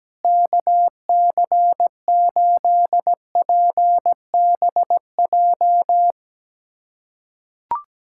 Morse code, heard as a keyed tone, about 17 wpm, 700 Hz. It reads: KC8PBJ
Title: USA Callsign DE Callsign 17wpm